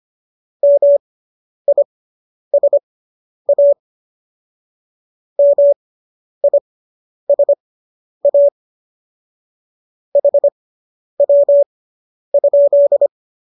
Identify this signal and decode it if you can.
Morse code, heard as a keyed tone, about 25 wpm, spaced as for 10 wpm, 570 Hz. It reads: MISA MISA HW?